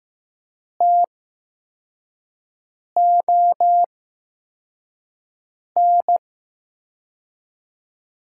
Morse code, heard as a keyed tone, about 15 wpm, 700 Hz. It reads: T O N
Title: USA Call Signs - 6 Characters - 8x Character Spacing - 15wpm